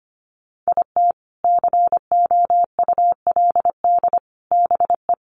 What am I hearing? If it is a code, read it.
Morse code, heard as a keyed tone, about 25 wpm, 700 Hz. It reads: IT COULD BE
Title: Sets of 3 Words 25wpm